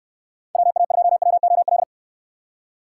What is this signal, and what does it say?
Morse code, heard as a keyed tone, about 68 wpm, 700 Hz. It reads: WA2WOL